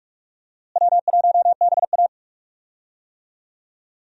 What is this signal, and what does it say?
Morse code, heard as a keyed tone, about 41 wpm, 700 Hz. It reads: W1BA
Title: USA Call Signs - 4 Characters - Repeated 2x - 45wpm